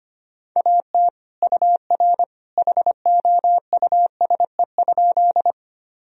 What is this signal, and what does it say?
Morse code, heard as a keyed tone, about 25 wpm, 700 Hz. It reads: AT UR HOUSE?
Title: Sentences from Top 200 Words and Abbreviations 25wpm